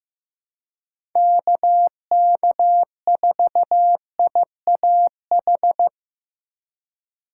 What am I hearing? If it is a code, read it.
Morse code, heard as a keyed tone, about 15 wpm, 700 Hz. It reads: KK4IAH
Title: USA Call Signs - 6 Characters - 4x Character Spacing - 15wpm